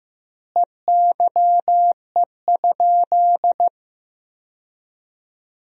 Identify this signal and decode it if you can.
Morse code, heard as a keyed tone, about 15 wpm, 700 Hz. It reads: EYE?
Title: 2x Word-Spacing - Sentences from Top 500 Words - 15wpm